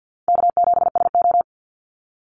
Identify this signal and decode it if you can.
Morse code, heard as a keyed tone, about 48 wpm, 700 Hz. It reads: K7SO
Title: NA SOTA Activator Callsigns 50wpm